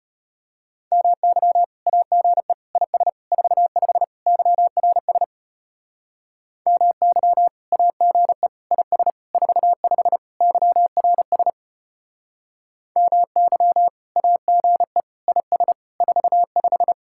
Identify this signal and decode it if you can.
Morse code, heard as a keyed tone, about 31 wpm, 700 Hz. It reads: MY AGE IS 45 YRS MY AGE IS 45 YRS MY AGE IS 45
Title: Speed-Racing Common QSO Phrases - 25wpm